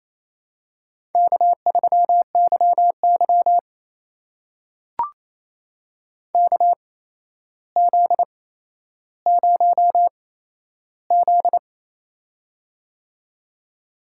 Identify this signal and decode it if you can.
Morse code, heard as a keyed tone, about 29 wpm, spaced as for 12 wpm, 700 Hz. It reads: K3YY KZ0Z
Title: USA Call Signs - 4 Characters - 8x Character Spacing - 28wpm